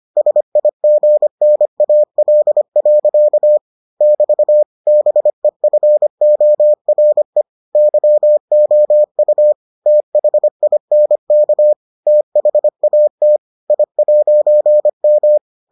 Morse code, heard as a keyed tone, about 25 wpm, 600 Hz. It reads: SIGNAL. <BT> BEFORE YOU THINK THAT I'M